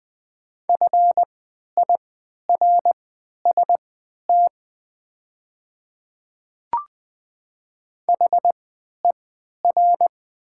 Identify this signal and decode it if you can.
Morse code, heard as a keyed tone, about 20 wpm, spaced as for 11 wpm, 700 Hz. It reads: FIRST HER
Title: 3x Character Spacing - Top 200 Words - 20wpm